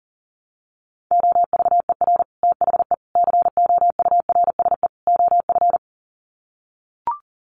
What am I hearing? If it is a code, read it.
Morse code, heard as a keyed tone, about 38 wpm, 700 Hz. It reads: OVER THE COURSE OF
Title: Sets of 4 Words 40wpm